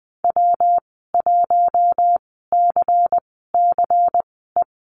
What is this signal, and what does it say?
Morse code, heard as a keyed tone, about 20 wpm, spaced as for 15 wpm, 700 Hz. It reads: W1CCE
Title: 128 Lesson 21g - Number 6 - Call Signs 20wpm